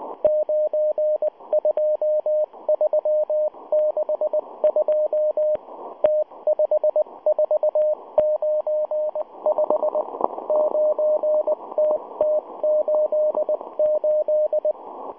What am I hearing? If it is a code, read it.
Morse code, heard as a keyed tone, about 19 wpm, 610 Hz. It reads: 92362 T5495 9TT88